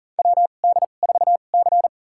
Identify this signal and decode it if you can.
Morse code, heard as a keyed tone, about 40 wpm, spaced as for 29 wpm, 700 Hz. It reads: WD4C
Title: USA Call Signs - 2x Character Spacing - 40wpm